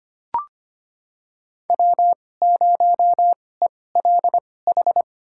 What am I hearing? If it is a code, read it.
Morse code, heard as a keyed tone, about 25 wpm, spaced as for 18 wpm, 700 Hz. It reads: W0ELH
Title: USA Call Signs - 5 Characters - 2x Character Spacing - 25wpm